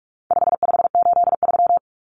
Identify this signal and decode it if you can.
Morse code, heard as a keyed tone, about 44 wpm, 700 Hz. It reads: 5583
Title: Common Cut Numbers - 4-Digit 45wpm